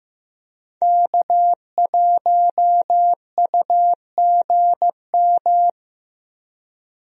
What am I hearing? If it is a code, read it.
Morse code, heard as a keyed tone, about 15 wpm, 700 Hz. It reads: K1UGM
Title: USA Call Signs - 5 Characters - 15wpm